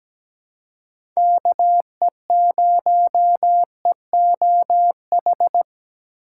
Morse code, heard as a keyed tone, about 17 wpm, 700 Hz. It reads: KE0EOH
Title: USA Call Signs - 6 Characters - 2x Character Spacing - 17wpm